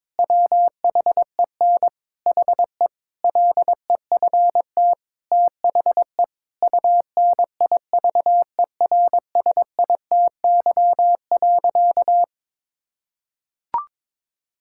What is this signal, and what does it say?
Morse code, heard as a keyed tone, about 22 wpm, 700 Hz. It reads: WHEN HE LEFT THE UNIVERSITY.
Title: Sentences from Top 1000 Words - Review Part 5 - 22wpm